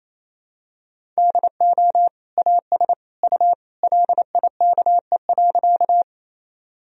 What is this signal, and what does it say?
Morse code, heard as a keyed tone, about 28 wpm, 700 Hz. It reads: DO AS U LIKE.
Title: Sentences from Top 100 Words and Abbreviations 28wpm